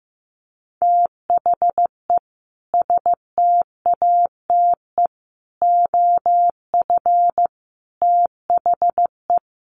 Morse code, heard as a keyed tone, about 15 wpm, 700 Hz. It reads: THE STATE OF THE
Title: Sentences from Top 1400 Words 15wpm